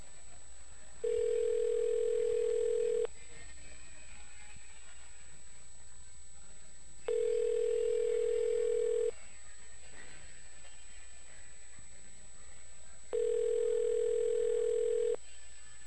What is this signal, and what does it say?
Ringback tone (North America), 440 + 480 Hz.